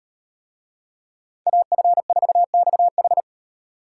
Morse code, heard as a keyed tone, about 38 wpm, 700 Hz. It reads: AF4XH